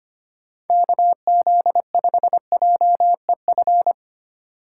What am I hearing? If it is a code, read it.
Morse code, heard as a keyed tone, about 25 wpm, 700 Hz. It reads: KZ5JEF
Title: USA Call Signs - 6 Characters - 4x Character Spacing - 25wpm